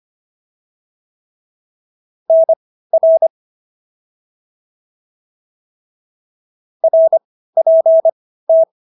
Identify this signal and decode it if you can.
Morse code, heard as a keyed tone, about 25 wpm, spaced as for 15 wpm, 650 Hz. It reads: NR RPT